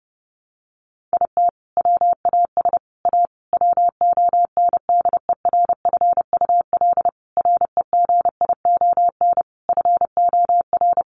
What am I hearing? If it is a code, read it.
Morse code, heard as a keyed tone, about 30 wpm, 700 Hz. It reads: IT WAS A WONDERFUL REGION FOR
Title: Sentences from Top 1500 Words 30wpm